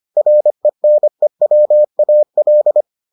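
Morse code, heard as a keyed tone, about 25 wpm, 600 Hz. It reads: RENEWAL
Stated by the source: Morse edition of the Foundations of Amateur Radio episodes.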